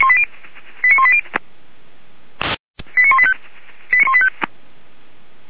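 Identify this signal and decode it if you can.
Five-tone selective call (ZVEI), heard as quick runs of five tones, tones of about 70 ms. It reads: 89180 89185
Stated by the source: ZVEI five-tone call received on SDR.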